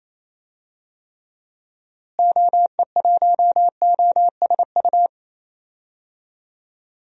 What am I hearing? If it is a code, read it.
Morse code, heard as a keyed tone, about 27 wpm, 700 Hz. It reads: OE1OSU
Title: DXCC-100 Call Signs - 6 Characters - Repeated 2x - 28wpm